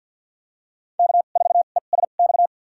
Morse code, heard as a keyed tone, about 53 wpm, spaced as for 37 wpm, 700 Hz. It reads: K4ESX
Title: USA Call Signs - 5 Characters - 2x Character Spacing - 50wpm